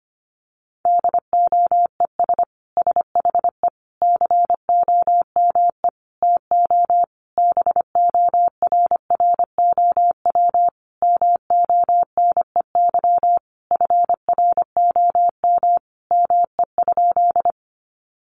Morse code, heard as a keyed tone, about 25 wpm, 700 Hz. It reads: DOES SHE COME TO BORROW MONEY FROM ME?
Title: Sentences from Top 1500 Words 25wpm